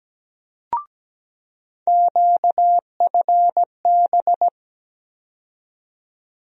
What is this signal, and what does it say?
Morse code, heard as a keyed tone, about 17 wpm, 700 Hz. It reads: QFB
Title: USA Call Sign Postfixes - 3 Letters 17wpm